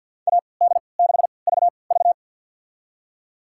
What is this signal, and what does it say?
Morse code, heard as a keyed tone, about 53 wpm, spaced as for 29 wpm, 700 Hz. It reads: AD6VV
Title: USA Call Signs - 3x Character Spacing - 50wpm